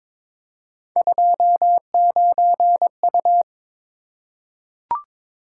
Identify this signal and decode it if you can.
Morse code, heard as a keyed tone, about 22 wpm, 700 Hz. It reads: 29U
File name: practice / 22wpm / Three Letter-Number Combinations - 22wpm